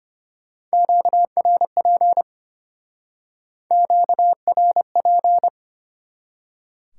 Morse code, heard as a keyed tone, about 33 wpm, 700 Hz. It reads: QRP QRP